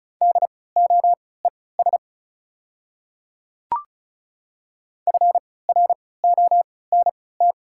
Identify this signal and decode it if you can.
Morse code, heard as a keyed tone, about 36 wpm, spaced as for 20 wpm, 700 Hz. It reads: DOES FRONT
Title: 3x Character Spacing - Top 500 Words - 35wpm